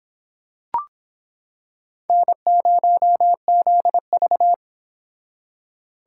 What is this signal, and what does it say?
Morse code, heard as a keyed tone, about 25 wpm, 700 Hz. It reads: N0ZV